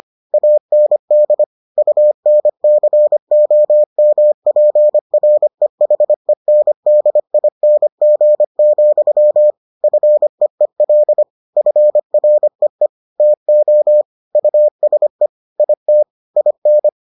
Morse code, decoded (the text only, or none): AND UNCOMPREHENDING, FEEL FREE TO USE IT IN